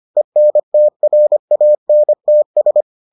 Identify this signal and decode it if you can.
Morse code, heard as a keyed tone, about 25 wpm, 600 Hz. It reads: ENTRANTS